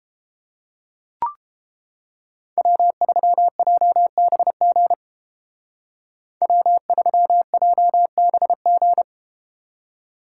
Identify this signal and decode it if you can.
Morse code, heard as a keyed tone, about 32 wpm, 700 Hz. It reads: W3JBG W3JBG